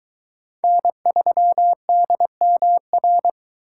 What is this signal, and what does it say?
Morse code, heard as a keyed tone, about 22 wpm, 700 Hz. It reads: N3DMR